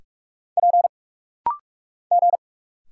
Morse code, heard as a keyed tone, about 46 wpm, 690 Hz. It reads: P G
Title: Single Letters - Rapid-Fire 45wpm